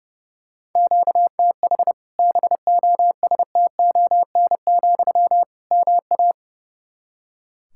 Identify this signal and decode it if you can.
Morse code, heard as a keyed tone, about 32 wpm, 700 Hz. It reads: QTH BOSTON, MA
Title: Practice QSO - Exchange 1 - Beginning - 30wpm